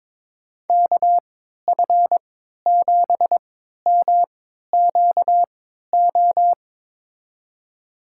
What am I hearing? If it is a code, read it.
Morse code, heard as a keyed tone, about 22 wpm, spaced as for 13 wpm, 700 Hz. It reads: KF7MQO
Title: USA Call Signs - 6 Characters - 3x Character Spacing - 22wpm